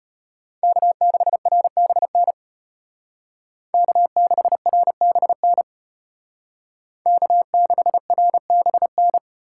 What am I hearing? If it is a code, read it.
Morse code, heard as a keyed tone, about 31 wpm, 700 Hz. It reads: K6RBN K6RBN K6RBN